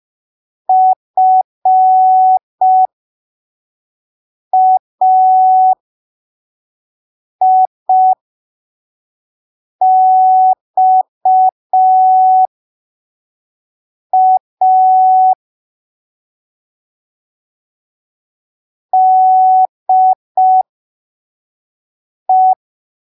Morse code, heard as a keyed tone, about 5 wpm, spaced as for 3 wpm, 750 Hz. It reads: FAIXA DE
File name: arquivos / texto / texto 12 05ppm